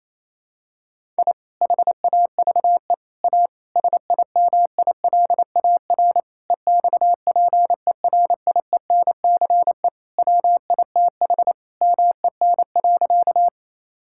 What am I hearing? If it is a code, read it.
Morse code, heard as a keyed tone, about 28 wpm, 700 Hz. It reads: I HAVE A SIMILAR EXPERIENCE WITH MEN.
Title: Sentences from Top 1000 Words - Review Part 5 - 28wpm